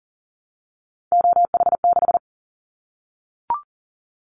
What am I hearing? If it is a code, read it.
Morse code, heard as a keyed tone, about 42 wpm, 700 Hz. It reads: OH6
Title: Callsign Prefixes for Top 50 DXCC - 40wpm